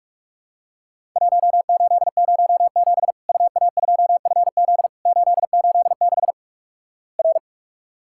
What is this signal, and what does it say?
Morse code, heard as a keyed tone, about 46 wpm, 700 Hz. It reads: 1807 UA2FZ 886 R
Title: Worked All Europe DX Contest - QTC Exchange - 45wpm